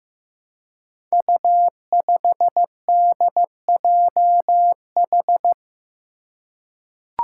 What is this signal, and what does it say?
Morse code, heard as a keyed tone, about 15 wpm, 700 Hz. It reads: U5DJH
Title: Five Challenging Letter-Number Combinations 15wpm